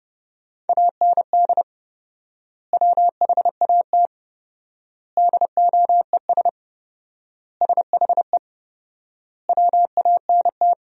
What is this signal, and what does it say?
Morse code, heard as a keyed tone, about 30 wpm, 700 Hz. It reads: AND WHAT DOES SHE WANT
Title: 4x Word-Spacing - Sentences from Top 400 Words - 30wpm